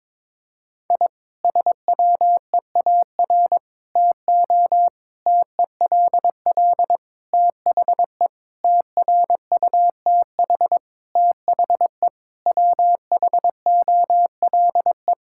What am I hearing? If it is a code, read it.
Morse code, heard as a keyed tone, about 22 wpm, 700 Hz. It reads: I SWEAR TO TELL THE TRUTH THE WHOLE
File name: Sentences with 9-Word Phrases - 22wpm